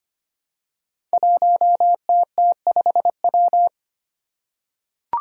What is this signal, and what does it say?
Morse code, heard as a keyed tone, about 25 wpm, 700 Hz. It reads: 1TT5W
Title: Five Challenging Letter-Number Combinations 25wpm